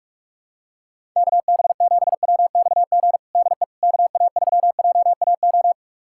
Morse code, heard as a keyed tone, about 45 wpm, 700 Hz. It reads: KB7WXG DE KA3JAO
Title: USA Callsign DE Callsign 45wpm